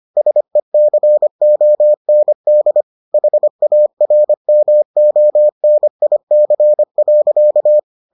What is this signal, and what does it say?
Morse code, heard as a keyed tone, about 25 wpm, 600 Hz. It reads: SECOND HARMONIC.